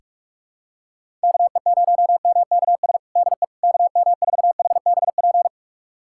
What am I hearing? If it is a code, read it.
Morse code, heard as a keyed tone, about 42 wpm, 700 Hz. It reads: KE0MKS DE KM4HBP
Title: USA Callsign DE Callsign - 6 Characters - 45wpm